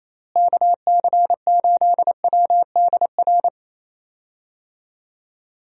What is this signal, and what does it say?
Morse code, heard as a keyed tone, about 28 wpm, 700 Hz. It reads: KC8WDR
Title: USA Call Signs - 6 Characters - Repeated 2x - 28wpm